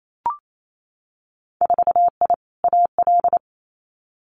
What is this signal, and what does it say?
Morse code, heard as a keyed tone, about 28 wpm, 700 Hz. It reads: 4I AL
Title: Winter Field Day Exchanges Encore 28wpm